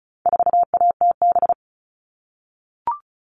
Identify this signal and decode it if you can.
Morse code, heard as a keyed tone, about 34 wpm, 700 Hz. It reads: 4ATB
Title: All Cut Numbers - 4-Digit 35wpm